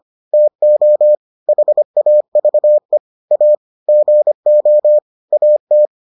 Morse code, heard as a keyed tone, about 25 wpm, 600 Hz. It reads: TO HAVE A GO AT